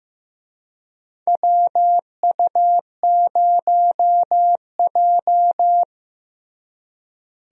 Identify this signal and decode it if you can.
Morse code, heard as a keyed tone, about 15 wpm, 700 Hz. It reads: WU0J
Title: USA Call Signs - 4 Characters - Repeated 3x - 15wpm